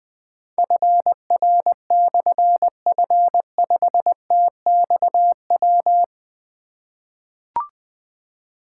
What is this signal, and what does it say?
Morse code, heard as a keyed tone, about 21 wpm, 700 Hz. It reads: FR/F5TXW
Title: Difficult Call Signs 20wpm